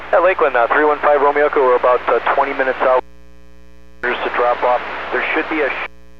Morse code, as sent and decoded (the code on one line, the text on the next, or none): none